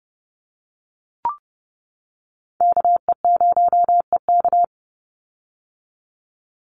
Morse code, heard as a keyed tone, about 30 wpm, 700 Hz. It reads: KE0EK
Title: Call Signs - 30wpm